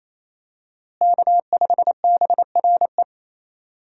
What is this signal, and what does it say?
Morse code, heard as a keyed tone, about 28 wpm, 700 Hz. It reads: K5BRE
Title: USA Call Signs - 5 Characters - Repeated 3x - 28wpm